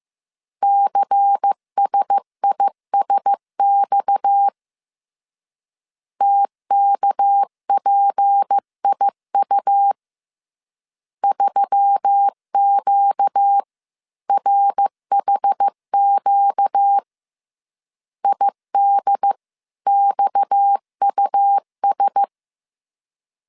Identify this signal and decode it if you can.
Morse code, heard as a keyed tone, about 15 wpm, 800 Hz. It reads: CSISX TKPIU 3QRHQ IDXUS
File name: CW / DET / 1 6 1500C S B 14wpm 2958